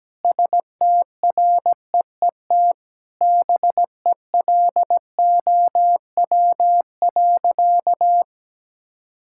Morse code, heard as a keyed tone, about 17 wpm, 700 Hz. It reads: STREET BELOW.
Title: Sentences from Top 1000 Words - Review Part 5 - 17wpm